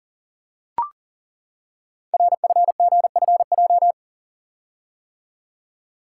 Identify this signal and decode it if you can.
Morse code, heard as a keyed tone, about 38 wpm, 700 Hz. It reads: RFGFJ
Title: Code Groups - 5 Letters - 40wpm